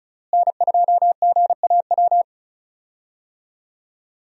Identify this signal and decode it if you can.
Morse code, heard as a keyed tone, about 34 wpm, 700 Hz. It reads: N2GAW
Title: USA Call Signs - 5 Characters - Repeated 3x - 35wpm